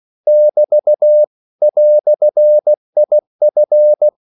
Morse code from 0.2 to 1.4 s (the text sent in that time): <BT>